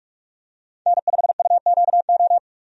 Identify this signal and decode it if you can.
Morse code, heard as a keyed tone, about 46 wpm, 700 Hz. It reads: N5UQO
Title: USA Call Signs - 5 Characters - 45wpm